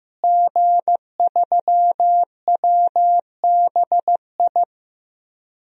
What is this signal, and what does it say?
Morse code, heard as a keyed tone, about 15 wpm, 700 Hz. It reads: G3WBI
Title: DXCC-100 Call Signs - 5 Characters - Repeated 3x - 15wpm